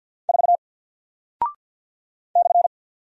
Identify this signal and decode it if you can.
Morse code, heard as a keyed tone, about 48 wpm, 700 Hz. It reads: 4 /
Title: Single Character (Letter:Number:Punctuation:Prosign) Alt 50wpm